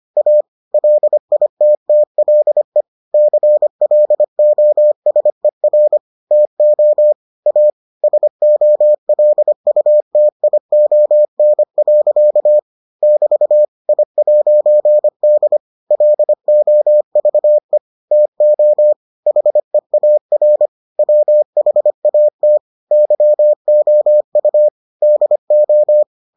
Morse code, heard as a keyed tone, about 25 wpm, 600 Hz. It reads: A LITTLE CLOSER TO A SOLUTION. <BT> I'D LOVE TO HEAR WHAT YOU DO